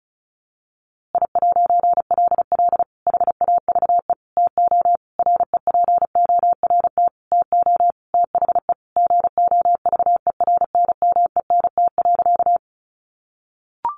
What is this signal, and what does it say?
Morse code, heard as a keyed tone, about 34 wpm, 700 Hz. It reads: I'LL HAVE TO REPORT TO THE GOVERNMENT.
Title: A Sound Of Thunder 35wpm